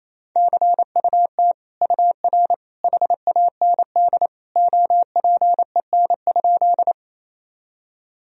Morse code, heard as a keyed tone, about 28 wpm, 700 Hz. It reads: CUT UR HAND OPEN?
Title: Sentences from Top 300 Words and Abbreviations 28wpm